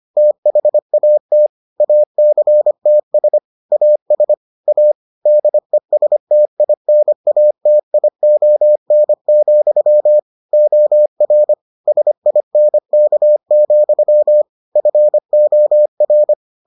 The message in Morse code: THAT ACTS AS A DESTINATION, OR SINK, FOR

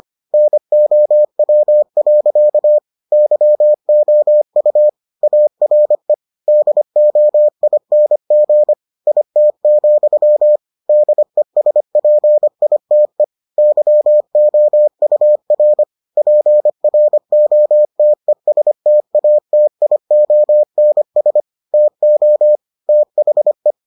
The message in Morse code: NOW. YOU ARE DOING IT, DESPITE YOUR PROTESTATIONS TO THE

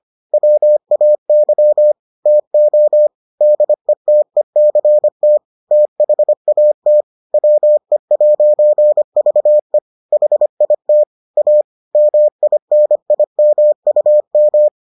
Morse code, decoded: WAY TO DETECT THAT WE'VE HIT A MINIMUM

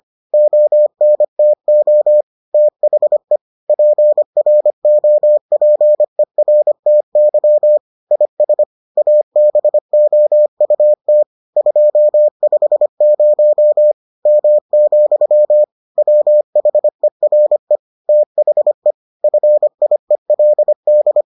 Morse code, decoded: ONTO THE PROPERTY IS ABOUT 250 M, WHERE THE FIELD

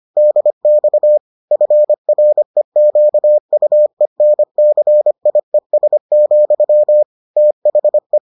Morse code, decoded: DX FREQUENCIES, THE

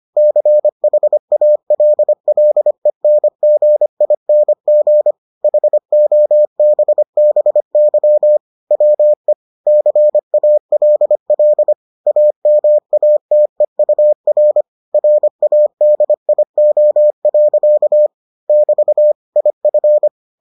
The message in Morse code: CHALLENGING HOBBY WE CALL AMATEUR RADIO. <BT> IF

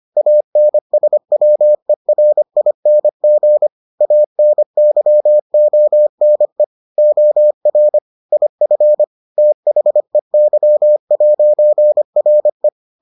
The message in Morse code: ANSWERING ANYONE OR IF THEY'RE

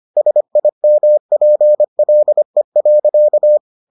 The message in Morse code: SIMPLE.

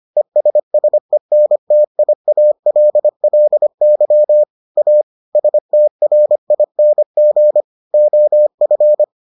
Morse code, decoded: ESSENTIALLY A STRING OF

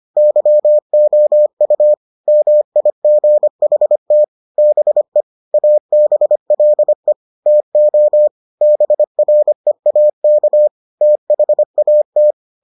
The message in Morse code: YOU MIGHT BE ABLE TO BREAK THAT

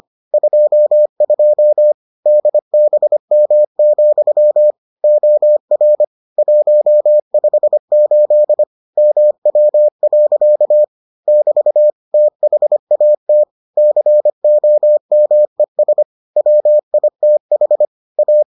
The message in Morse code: 22 DBM, OR 158 MW. <BT> THAT COMES WITH A

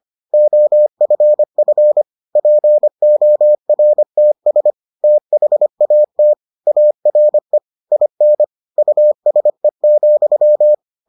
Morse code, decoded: OFF PORTS THAT ARE IN USE,

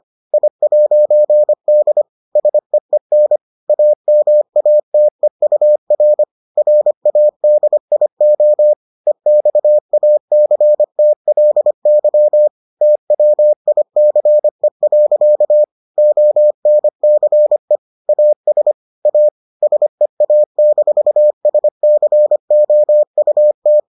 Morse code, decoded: I'D SEEN AMATEUR RADIO EXACTLY TWICE. ONCE AS A SEA-SCOUT

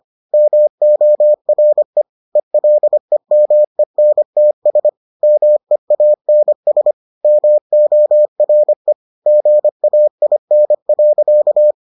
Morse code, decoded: MORE ELEMENTS MEANS MORE GAIN.

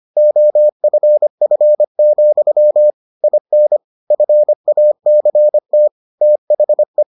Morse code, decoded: OFF, IN FACT THE

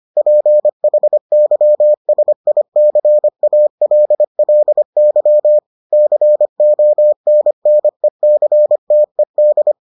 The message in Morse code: PHYSICALLY CONNECTED

